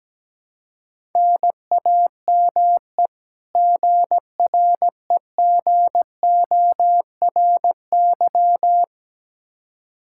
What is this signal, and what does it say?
Morse code, heard as a keyed tone, about 17 wpm, 700 Hz. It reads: NAME GREGORY
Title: Practice QSO - Exchange 1 - Beginning - 17wpm